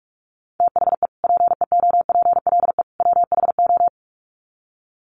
Morse code, heard as a keyed tone, about 44 wpm, 700 Hz. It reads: THE PEOPLE WHO
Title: Sets of 3 Words 45wpm